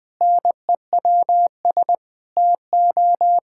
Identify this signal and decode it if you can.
Morse code, heard as a keyed tone, about 20 wpm, 700 Hz. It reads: NEWS TO
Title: Pangram Challenge v2 - 20wpm